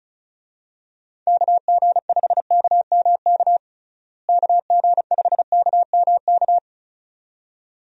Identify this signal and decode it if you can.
Morse code, heard as a keyed tone, about 34 wpm, 700 Hz. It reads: KG5KMK KG5KMK
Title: USA Call Signs - 6 Characters - Repeated 2x - 35wpm